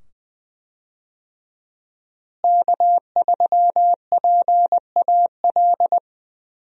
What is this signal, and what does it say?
Morse code, heard as a keyed tone, about 20 wpm, 700 Hz. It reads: K3PAL
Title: USA Call Signs - 5 Characters - Repeated 2x - 20wpm